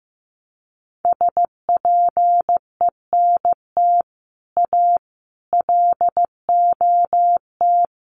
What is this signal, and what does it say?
Morse code, heard as a keyed tone, about 15 wpm, 700 Hz. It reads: SPENT A LOT
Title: Sets of 5 Words 15wpm